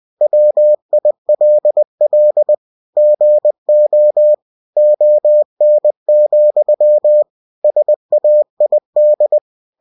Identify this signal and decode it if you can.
Morse code, heard as a keyed tone, about 20 wpm, 600 Hz. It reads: WILL GO ON, SAID
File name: cw-books / winnie-the-pooh / winnie-20wpm-page-0002